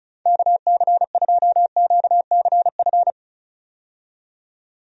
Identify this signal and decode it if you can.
Morse code, heard as a keyed tone, about 36 wpm, 700 Hz. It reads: KC2QCF
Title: USA Call Signs - 6 Characters - 35wpm